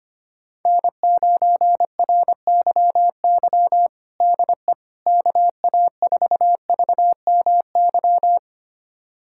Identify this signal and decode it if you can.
Morse code, heard as a keyed tone, about 25 wpm, 700 Hz. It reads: N9RYY DE KA4VMY
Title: USA Callsign DE Callsign 25wpm